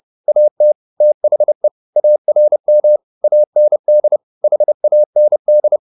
Morse code, heard as a keyed tone, about 30 wpm, 600 Hz. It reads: AT THE ARM AND HAND
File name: the old man and the sea 2 30wpm